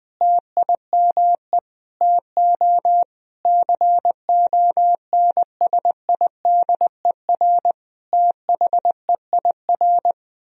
Morse code, heard as a keyed tone, about 20 wpm, 700 Hz. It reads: TIME TO CONSIDER THEIR
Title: Sentences from Top 1000 Words - Review Part 4 - 20wpm